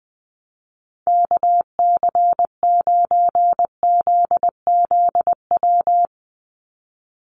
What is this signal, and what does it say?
Morse code, heard as a keyed tone, about 20 wpm, 700 Hz. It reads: KC9ZZW
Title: USA Call Signs 20wpm